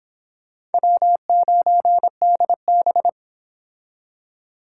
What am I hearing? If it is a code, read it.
Morse code, heard as a keyed tone, about 30 wpm, 700 Hz. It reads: W9DB